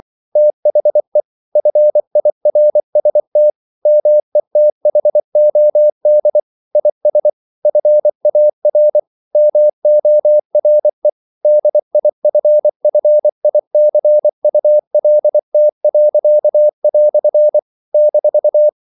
Morse code, heard as a keyed tone, about 24 wpm, 600 Hz. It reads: THE FIRST METHOD IS FAR MORE DIFFICULT." -
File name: CW / Practice / EnglishQuotations0224